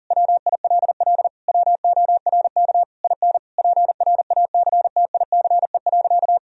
Morse code, heard as a keyed tone, about 40 wpm, 700 Hz. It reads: WILL WORK IN PRACTICE.
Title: Sentences from Top 1000 Words - Review Part 4 - 40wpm